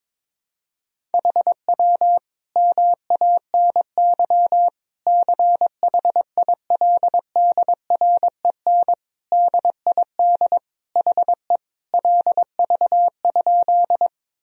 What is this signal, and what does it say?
Morse code, heard as a keyed tone, about 22 wpm, 700 Hz. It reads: HW MANY CHILDREN DID HE LV?
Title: Sentences from Top 400 Words and Abbreviations 22wpm